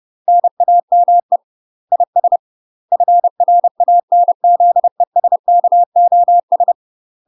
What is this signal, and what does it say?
Morse code, heard as a keyed tone, about 30 wpm, 700 Hz. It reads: NAME IS FRANZESKOS
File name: UK MASTER QSOs TRAINING / 30WPM / UK MASTER QSOs 0015